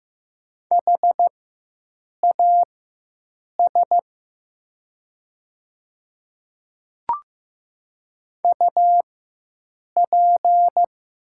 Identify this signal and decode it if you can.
Morse code, heard as a keyed tone, about 15 wpm, spaced as for 7 wpm, 700 Hz. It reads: HAS UP
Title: 4x Character Spacing - Top 100 Words - 15wpm